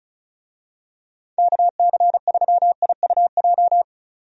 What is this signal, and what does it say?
Morse code, heard as a keyed tone, about 36 wpm, 700 Hz. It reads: KC3IUJ
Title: USA Call Signs - 6 Characters - 35wpm